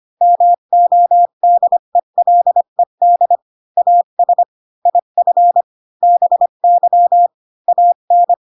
Morse code, heard as a keyed tone, about 25 wpm, 700 Hz. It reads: MODELED AS IF BY AN
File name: Pinocchio / Pinocchio ch8 20 25 ik1hge